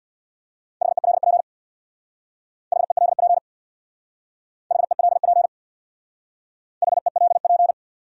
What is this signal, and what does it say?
Morse code, heard as a keyed tone, about 60 wpm, 700 Hz. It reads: HELP HELP HELP HELP
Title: Speed-Racing Top 500 Words 50wpm